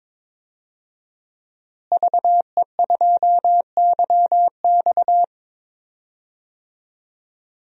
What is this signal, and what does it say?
Morse code, heard as a keyed tone, about 22 wpm, 700 Hz. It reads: VE2YX
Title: DXCC-100 Call Signs - 5 Characters - Repeated 2x - 22wpm